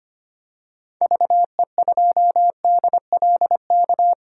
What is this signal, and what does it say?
Morse code, heard as a keyed tone, about 25 wpm, 700 Hz. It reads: VE2DLK
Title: DXCC-100 Call Signs - 6 Characters - 25wpm